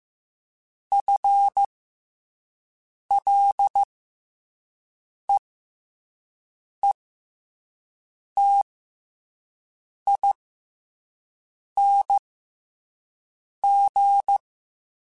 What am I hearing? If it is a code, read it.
Morse code, heard as a keyed tone, about 15 wpm, spaced as for 5 wpm, 800 Hz. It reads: FLEETING
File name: FL / FL mp3s / FL Prefixes 3x2 @5